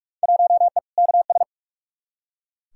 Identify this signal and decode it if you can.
Morse code, heard as a keyed tone, about 46 wpm, 700 Hz. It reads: 1E KS
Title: Field Day Exchanges - 45wpm